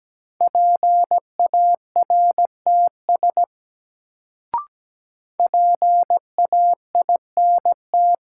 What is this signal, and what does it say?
Morse code, heard as a keyed tone, about 17 wpm, 700 Hz. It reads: PARTS PAINT
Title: Sentences from Top 1000 Words - Review Part 6 - New Word Drill - 17wpm